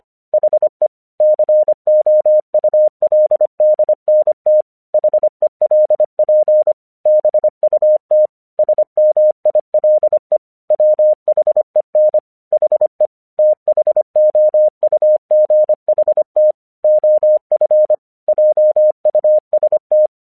Morse code, heard as a keyed tone, about 25 wpm, 610 Hz. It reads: HE COULDNT HELP BUT SMILE WHEN HE THOUGHT OF JUST
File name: 1 English 25wpm / AQC32